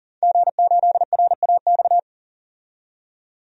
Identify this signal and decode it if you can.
Morse code, heard as a keyed tone, about 40 wpm, 700 Hz. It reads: G8RAX
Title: DXCC-100 Call Signs - 5 Characters - 40wpm